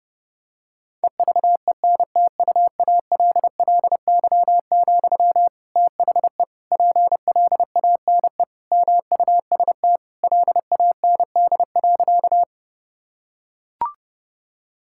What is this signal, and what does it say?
Morse code, heard as a keyed tone, about 30 wpm, 700 Hz. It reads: EVENTUALLY, THE PLANE MUST LAND.
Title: Sentences from Top 1000 Words - Review Part 6 - 30wpm